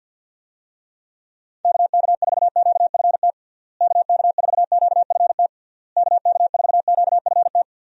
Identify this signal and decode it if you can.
Morse code, heard as a keyed tone, about 50 wpm, 700 Hz. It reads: KK4QFT KK4QFT KK4QFT
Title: USA Call Signs - 6 Characters - Repeated 3x - 50wpm